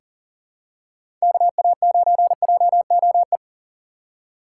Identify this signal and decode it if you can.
Morse code, heard as a keyed tone, about 37 wpm, 700 Hz. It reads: KA9JOE
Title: USA Call Signs - 3x Character Spacing - 40wpm